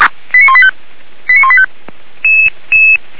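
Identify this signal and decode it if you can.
Five-tone selective call (ZVEI), heard as quick runs of five tones, tones of about 70 ms. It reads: 89185 89185
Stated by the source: ZVEI five-tone selective call 89180, received by SDR.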